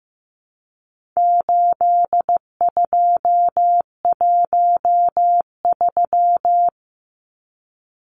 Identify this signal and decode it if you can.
Morse code, heard as a keyed tone, about 15 wpm, 700 Hz. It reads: 8213
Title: Numbers - 4-Digit 15wpm